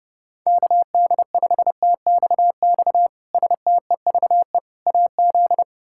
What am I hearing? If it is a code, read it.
Morse code, heard as a keyed tone, about 32 wpm, 700 Hz. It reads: KD5TXX STEVE AZ
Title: K1USN SST QSO Practice - 30wpm